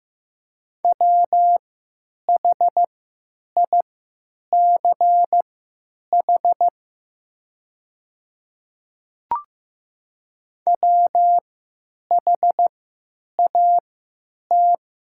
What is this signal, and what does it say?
Morse code, heard as a keyed tone, about 15 wpm, spaced as for 8 wpm, 700 Hz. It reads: WHICH WHAT
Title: 3x Character Spacing - Top 100 Words - 15wpm